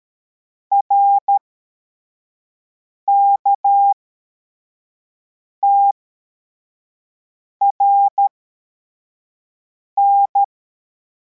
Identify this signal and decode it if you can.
Morse code, heard as a keyed tone, about 13 wpm, spaced as for 4 wpm, 800 Hz. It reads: RKTRN
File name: formation / cw / Text11